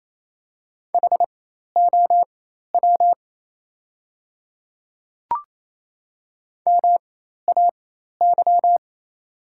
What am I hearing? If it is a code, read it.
Morse code, heard as a keyed tone, about 28 wpm, spaced as for 13 wpm, 700 Hz. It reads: HOW MAY
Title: 4x Character Spacing - Top 100 Words - 28wpm